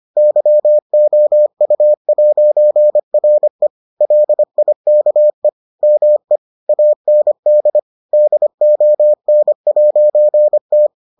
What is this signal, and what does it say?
Morse code, heard as a keyed tone, about 25 wpm, 600 Hz. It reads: YOU'RE LIKE ME AND DON'T